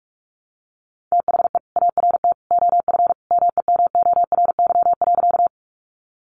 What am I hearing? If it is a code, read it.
Morse code, heard as a keyed tone, about 44 wpm, 700 Hz. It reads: THE ART OF MEMORY.
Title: Sentences from Top 1300 Words 45wpm